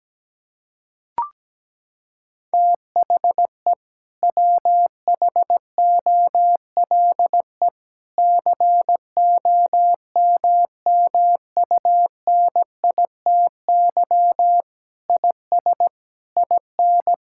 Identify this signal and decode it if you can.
Morse code, heard as a keyed tone, about 17 wpm, 700 Hz. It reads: THE WHOLE COMMUNITY IS IN
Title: Sentences from Top 1600 Words 17wpm